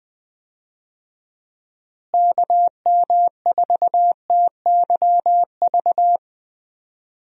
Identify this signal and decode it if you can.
Morse code, heard as a keyed tone, about 20 wpm, 700 Hz. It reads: KM4TYV
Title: USA Call Signs - 6 Characters - Repeated 3x - 20wpm